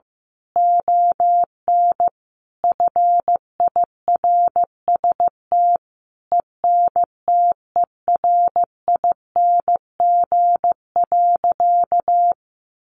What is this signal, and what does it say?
Morse code, heard as a keyed tone, about 15 wpm, 700 Hz. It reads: ON FIRST ENTERING.
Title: A Sound Of Thunder 15wpm